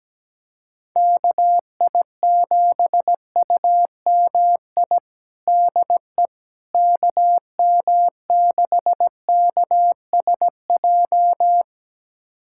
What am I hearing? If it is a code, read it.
Morse code, heard as a keyed tone, about 17 wpm, 700 Hz. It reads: KI7UMI DE KM6KSJ
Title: USA Callsign DE Callsign - 6 Characters - 17wpm